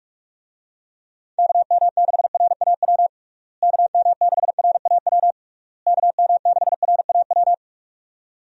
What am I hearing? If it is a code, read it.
Morse code, heard as a keyed tone, about 46 wpm, 700 Hz. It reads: KM6RAW KM6RAW KM6RAW
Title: USA Call Signs - 6 Characters - Repeated 3x - 45wpm